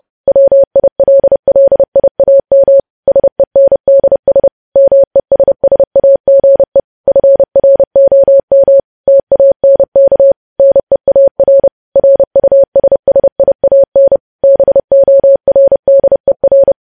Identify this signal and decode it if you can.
Morse code, heard as a keyed tone, about 30 wpm, 550 Hz. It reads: WILLIAM SENDS MESSAGE FROM TANK NEAR RUSSIAN BORDER
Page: news headlines in morse code at 30 wpm